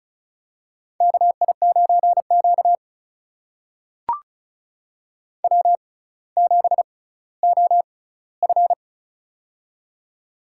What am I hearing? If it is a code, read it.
Morse code, heard as a keyed tone, about 36 wpm, spaced as for 18 wpm, 700 Hz. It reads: KI9Q W7OF
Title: USA Call Signs - 4 Characters - 6x Character Spacing - 35wpm